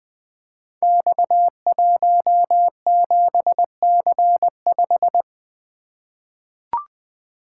Morse code, heard as a keyed tone, about 20 wpm, 700 Hz. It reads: X17C5
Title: Code Groups - 5 Letter:Numbers - 20wpm